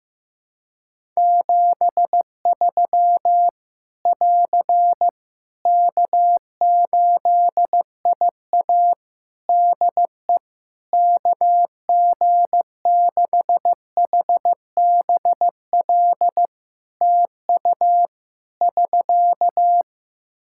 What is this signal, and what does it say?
Morse code, heard as a keyed tone, about 15 wpm, 700 Hz. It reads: 73 <AR> K8IA DE KG6HBL TU <SK>
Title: Practice QSO - Exchange 4 - Ending - 15wpm